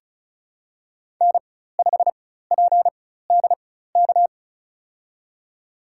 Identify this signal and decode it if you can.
Morse code, heard as a keyed tone, about 34 wpm, spaced as for 16 wpm, 700 Hz. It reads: N5PDK
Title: USA Call Signs - 5 Characters - 4x Character Spacing - 35wpm